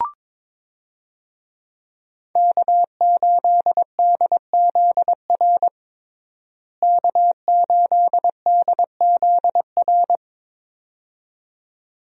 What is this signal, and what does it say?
Morse code, heard as a keyed tone, about 22 wpm, 700 Hz. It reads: K8DZR K8DZR
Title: USA Call Signs - 5 Characters - Repeated 2x - 22wpm